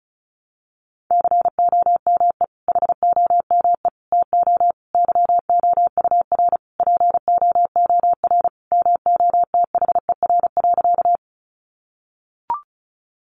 Morse code, heard as a keyed tone, about 34 wpm, 700 Hz. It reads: COME HOME TO YOUR POOR MOTHER.
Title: Sentences from Top 1000 Words 35wpm